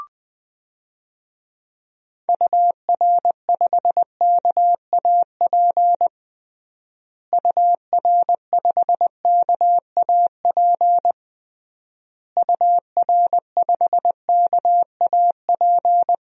Morse code, heard as a keyed tone, about 20 wpm, 700 Hz. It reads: UR5KAP UR5KAP UR5KAP
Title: DXCC-100 Call Signs - 6 Characters - Repeated 3x - 20wpm